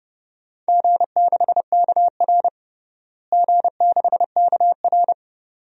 Morse code, heard as a keyed tone, about 30 wpm, 700 Hz. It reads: G6KR G6KR
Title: DXCC-100 Call Signs - 4 Characters - Repeated 2x - 30wpm